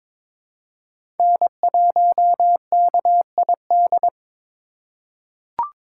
Morse code, heard as a keyed tone, about 21 wpm, 700 Hz. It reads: N1KID
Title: USA Call Signs - 5 Characters - 3x Character Spacing - 22wpm